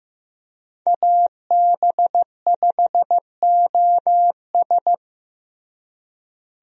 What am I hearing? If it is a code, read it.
Morse code, heard as a keyed tone, about 16 wpm, 700 Hz. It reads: AB5OS
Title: USA Call Signs - 5 Characters - Repeated 2x - 15wpm